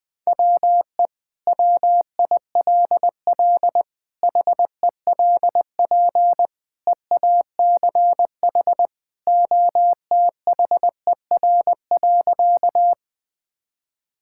Morse code, heard as a keyed tone, about 20 wpm, 700 Hz. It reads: WE WILL HELP EACH OTHER.
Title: Sentences with 2-Word Phrases - 20wpm